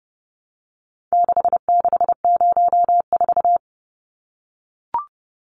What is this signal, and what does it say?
Morse code, heard as a keyed tone, about 30 wpm, 700 Hz. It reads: 6604
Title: Numbers - 4-Digit 30wpm